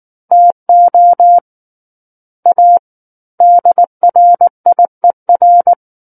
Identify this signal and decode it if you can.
Morse code, heard as a keyed tone, about 19 wpm, 700 Hz. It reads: TO A DRIER
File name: Book One / 10-Chapter 10 - 19 WPM